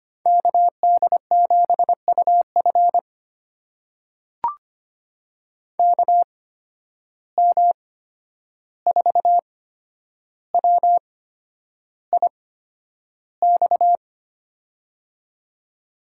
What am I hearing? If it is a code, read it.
Morse code, heard as a keyed tone, about 25 wpm, spaced as for 10 wpm, 700 Hz. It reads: KD7UF KM4WIX
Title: USA Call Signs - 8x Character Spacing - 25wpm